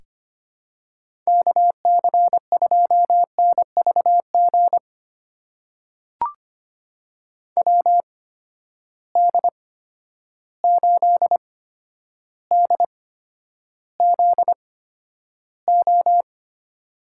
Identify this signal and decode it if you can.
Morse code, heard as a keyed tone, about 25 wpm, spaced as for 11 wpm, 700 Hz. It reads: KC2NVG WD8DZO
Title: USA Call Signs - 6 Characters - 8x Character Spacing - 25wpm